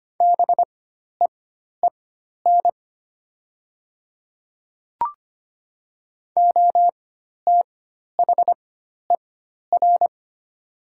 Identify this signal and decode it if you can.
Morse code, heard as a keyed tone, about 25 wpm, spaced as for 12 wpm, 700 Hz. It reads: BEEN OTHER
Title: 4x Character Spacing - Top 100 Words - 25wpm